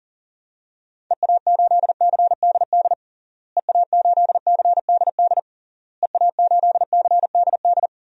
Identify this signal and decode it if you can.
Morse code, heard as a keyed tone, about 37 wpm, 700 Hz. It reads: EA8CDD EA8CDD EA8CDD
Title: DXCC-100 Call Signs - 6 Characters - Repeated 3x - 40wpm